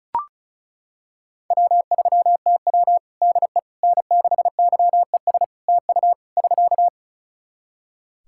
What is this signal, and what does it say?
Morse code, heard as a keyed tone, about 34 wpm, 700 Hz. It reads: W3TW DE N6YES TU <SK>
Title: Practice QSO - Exchange 4 - Ending - 35wpm